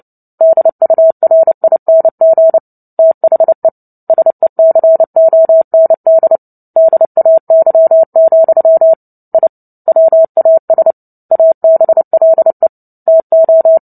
Morse code, heard as a keyed tone, about 30 wpm, 650 Hz. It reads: DURING THE SECOND DAY, I WAS ABLE TO
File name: Book Two / 03-Chapter 3 - 29 WPM